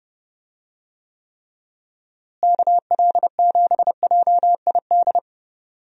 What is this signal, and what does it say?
Morse code, heard as a keyed tone, about 30 wpm, 700 Hz. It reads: KL7JID
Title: DXCC-100 Call Signs - 6 Characters - Repeated 2x - 30wpm